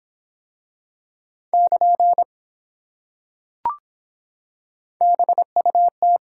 Morse code, heard as a keyed tone, about 23 wpm, 700 Hz. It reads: <KN> BUT